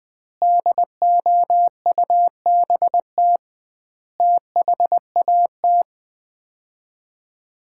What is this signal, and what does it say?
Morse code, heard as a keyed tone, about 21 wpm, 700 Hz. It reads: DOUBT THAT
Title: 2x Word-Spacing - Sets of 5 Words - 20wpm